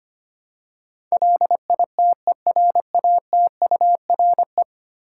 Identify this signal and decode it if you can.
Morse code, heard as a keyed tone, about 25 wpm, 700 Hz. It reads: LITERATURE
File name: practice / 25wpm / Ten Letter Words - 25wpm